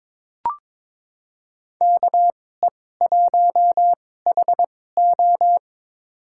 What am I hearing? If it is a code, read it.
Morse code, heard as a keyed tone, about 22 wpm, spaced as for 16 wpm, 700 Hz. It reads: KE1HO
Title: USA Call Signs - 5 Characters - 2x Character Spacing - 22wpm